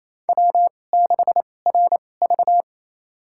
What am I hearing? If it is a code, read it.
Morse code, heard as a keyed tone, about 28 wpm, spaced as for 20 wpm, 700 Hz. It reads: W6RV
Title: USA Call Signs - 4 Characters - 2x Character Spacing - 28wpm